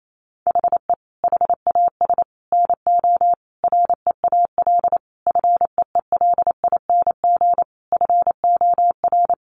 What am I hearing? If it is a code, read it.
Morse code, heard as a keyed tone, about 28 wpm, 700 Hz. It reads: HE HAS NO REAL FEELING FOR
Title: Sentences from Top 1200 Words 28wpm